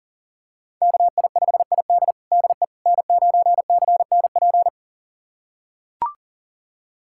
Morse code, heard as a keyed tone, about 40 wpm, 700 Hz. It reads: KI5ID DE N9CNP
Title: USA Callsign DE Callsign - 5 Characters - 40wpm